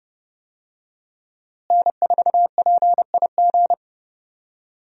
Morse code, heard as a keyed tone, about 31 wpm, 700 Hz. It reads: N4PIG
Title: USA Call Signs - 5 Characters - Repeated 2x - 30wpm